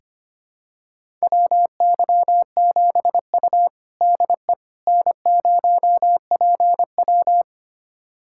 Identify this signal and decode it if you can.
Morse code, heard as a keyed tone, about 25 wpm, 700 Hz. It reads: WY7U DE N0PW
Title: USA Callsign DE Callsign - 4 Characters - 25wpm